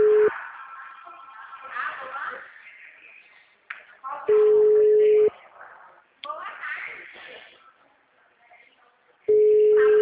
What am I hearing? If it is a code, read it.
Ringback tone (Europe, 425 Hz), with speech.